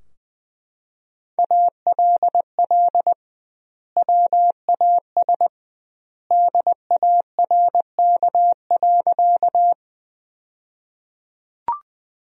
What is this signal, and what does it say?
Morse code, heard as a keyed tone, about 20 wpm, 700 Hz. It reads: ALL WAS DARK.
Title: 2x Word-Spacing - Sentences from Top 500 Words - 20wpm